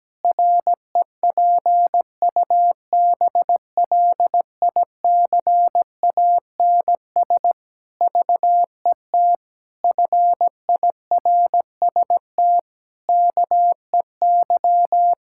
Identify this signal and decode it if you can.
Morse code, heard as a keyed tone, about 17 wpm, 700 Hz. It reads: REPUBLICANS VET FIRST KEY
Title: Pangram Challenge v2 - 17wpm